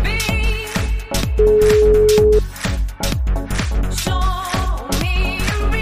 Ringback tone (Europe, 425 Hz), with noise.